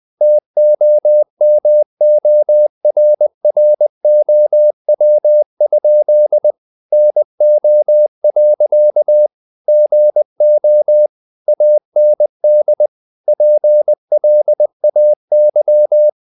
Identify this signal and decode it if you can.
Morse code, heard as a keyed tone, about 20 wpm, 600 Hz. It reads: TOMORROW? NO. GO AND PLAY
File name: the old man and the sea / the old man and the sea part 1 20wpm